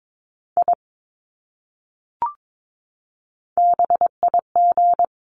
Morse code, heard as a keyed tone, about 22 wpm, 700 Hz. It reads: I BIG
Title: Top 300 Words Repeated 22wpm